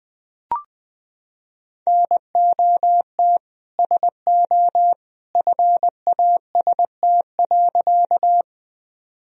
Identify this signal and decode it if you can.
Morse code, heard as a keyed tone, about 21 wpm, 700 Hz. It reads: NOT SO FAST.
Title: Sentences from Top 1000 Words - Review Part 2 - 20wpm